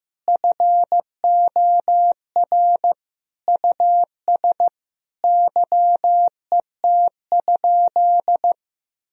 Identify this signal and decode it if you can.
Morse code, heard as a keyed tone, about 15 wpm, 700 Hz. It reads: FOR US YET?
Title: Sentences from Top 1000 Words - Review Part 3 - 15wpm